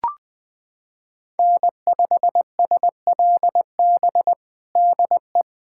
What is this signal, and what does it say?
Morse code, heard as a keyed tone, about 21 wpm, 700 Hz. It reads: N5SLB DE
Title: USA Callsign DE Callsign 20wpm